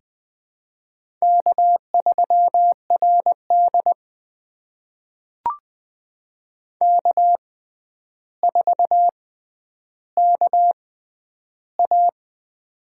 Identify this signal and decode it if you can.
Morse code, heard as a keyed tone, about 20 wpm, spaced as for 10 wpm, 700 Hz. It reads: K3RD K4KA
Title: USA Call Signs - 4 Characters - 6x Character Spacing - 20wpm